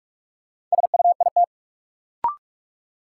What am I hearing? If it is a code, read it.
Morse code, heard as a keyed tone, about 45 wpm, 700 Hz. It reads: SUIT
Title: Sentences from Top 1600 Words - New Word Drill - 45wpm